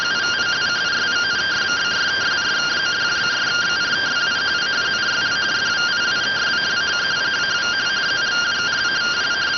Two-tone teleprinter signal, about 170 Hz shift.